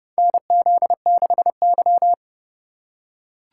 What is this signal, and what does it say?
Morse code, heard as a keyed tone, about 30 wpm, 700 Hz. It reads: NZ6Y
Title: USA Call Signs - 4 Characters - 2x Character Spacing - 30wpm